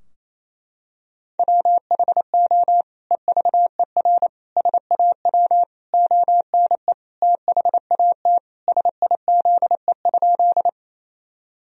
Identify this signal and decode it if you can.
Morse code, heard as a keyed tone, about 28 wpm, 700 Hz. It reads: WHO EVER SAW ONE THAT SIZE?
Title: Sentences from Top 1000 Words - Review Part 2 - 28wpm